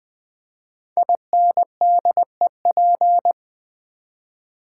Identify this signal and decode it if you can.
Morse code, heard as a keyed tone, about 20 wpm, 700 Hz. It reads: INDEP
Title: Build A Word - 5 Letters At A Time - 12-Letter Words - 20wpm